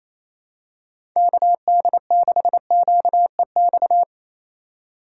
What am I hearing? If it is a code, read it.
Morse code, heard as a keyed tone, about 28 wpm, 700 Hz. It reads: KD6QEX
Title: USA Call Signs - 6 Characters - 4x Character Spacing - 28wpm